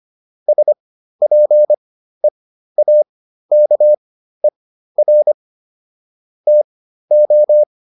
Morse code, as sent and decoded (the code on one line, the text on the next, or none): ... .--. .- -.- . .-. / - ---
SPEAKER TO